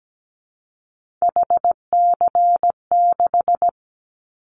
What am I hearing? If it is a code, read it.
Morse code, heard as a keyed tone, about 17 wpm, 700 Hz. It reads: HC6
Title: Callsign Prefixes for Top 100 DXCC - 17wpm